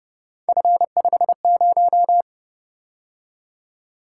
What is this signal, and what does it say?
Morse code, heard as a keyed tone, about 28 wpm, 700 Hz. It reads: F50